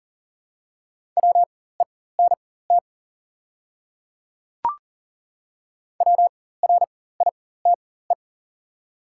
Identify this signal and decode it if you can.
Morse code, heard as a keyed tone, about 40 wpm, spaced as for 19 wpm, 700 Hz. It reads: WENT WRITE
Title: 4x Character Spacing - Top 300 Words - 40wpm